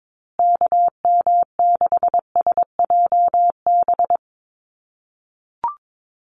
Morse code, heard as a keyed tone, about 22 wpm, 700 Hz. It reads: KM6SJB
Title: USA Call Signs Repeated 3x 22wpm